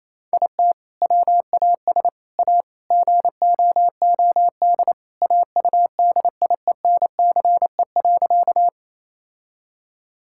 Morse code, heard as a keyed tone, about 28 wpm, 700 Hz. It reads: IT WAS A GOOD AUDIENCE.
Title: Sentences from Top 1000 Words - Review Part 5 - 28wpm